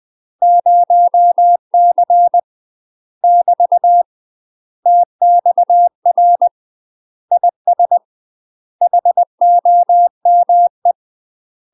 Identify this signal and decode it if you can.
Morse code, heard as a keyed tone, about 20 wpm, 700 Hz. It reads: 0C <BT> TXR IS HOME
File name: UK MASTER QSOs TRAINING / 20WPM / UK MASTER QSOs 0068